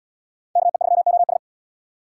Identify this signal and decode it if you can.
Morse code, heard as a keyed tone, about 72 wpm, 700 Hz. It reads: KE3OS